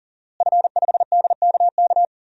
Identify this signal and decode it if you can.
Morse code, heard as a keyed tone, about 37 wpm, 700 Hz. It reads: F5DKK
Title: DXCC-100 Call Signs - 5 Characters - Repeated 3x - 40wpm